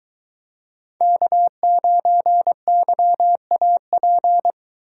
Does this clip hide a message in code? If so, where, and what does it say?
Yes: Morse code from 1.0 to 4.5 s, heard as a keyed tone, about 23 wpm, 700 Hz. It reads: K9YAP